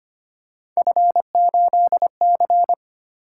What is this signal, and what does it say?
Morse code, heard as a keyed tone, about 25 wpm, 700 Hz. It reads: F8C
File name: Three Letter-Number Combinations - 25wpm